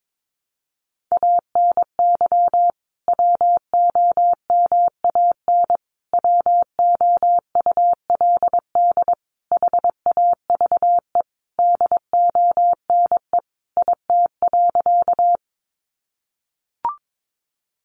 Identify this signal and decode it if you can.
Morse code, heard as a keyed tone, about 22 wpm, 700 Hz. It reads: ANY WOMAN WOULD HAVE DONE IT.
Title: Sentences from Top 1100 Words 22wpm